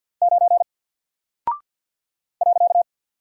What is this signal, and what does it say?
Morse code, heard as a keyed tone, about 48 wpm, 700 Hz. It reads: 9 .
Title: Single Character (Letter:Number:Punctuation:Prosign) 50wpm